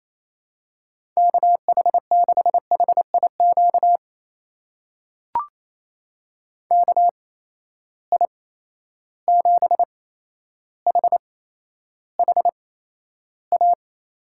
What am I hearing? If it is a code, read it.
Morse code, heard as a keyed tone, about 28 wpm, spaced as for 12 wpm, 700 Hz. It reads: KH6HIQ KI7HHA
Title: USA Call Signs - 6 Characters - 8x Character Spacing - 28wpm